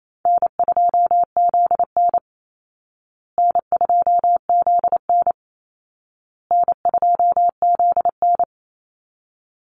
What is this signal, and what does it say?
Morse code, heard as a keyed tone, about 28 wpm, 700 Hz. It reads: N2ZN N2ZN N2ZN
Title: Call Signs Repeated 2x - 28wpm